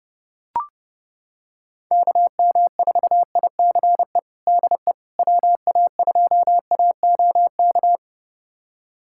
Morse code, heard as a keyed tone, about 30 wpm, 700 Hz. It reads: KM4ICE DE WA2AOK
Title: USA Callsign DE Callsign - 6 Characters - 30wpm